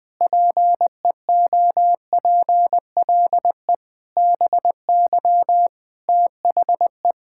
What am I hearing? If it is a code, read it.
Morse code, heard as a keyed tone, about 20 wpm, 700 Hz. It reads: PEOPLE BY THE
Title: Sets of 9 Words - 20wpm